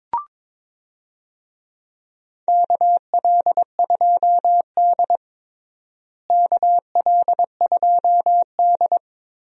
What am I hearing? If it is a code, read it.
Morse code, heard as a keyed tone, about 22 wpm, 700 Hz. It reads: KL2D KL2D
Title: USA Call Signs - 4 Characters - Repeated 2x - 22wpm